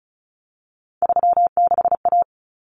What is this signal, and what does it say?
Morse code, heard as a keyed tone, about 34 wpm, 700 Hz. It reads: 36A
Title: Common Cut Numbers - 3-Digit 35wpm